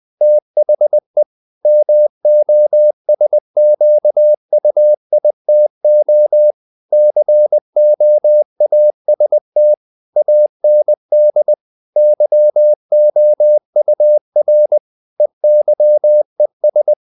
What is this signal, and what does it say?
Morse code, heard as a keyed tone, about 20 wpm, 600 Hz. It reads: THE MOSQUITO COAST AND YOUR EYES
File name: the old man and the sea / the old man and the sea part 1 20wpm